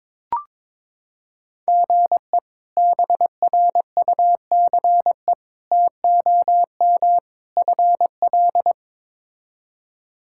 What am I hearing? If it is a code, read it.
Morse code, heard as a keyed tone, about 22 wpm, 700 Hz. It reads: GE BRUCE TOM FL
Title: K1USN SST QSO Practice - 22wpm